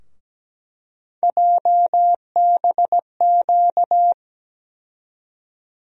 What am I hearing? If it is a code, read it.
Morse code, heard as a keyed tone, about 17 wpm, 700 Hz. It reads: JBQ
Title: USA Call Sign Postfixes - 3 Letters 17wpm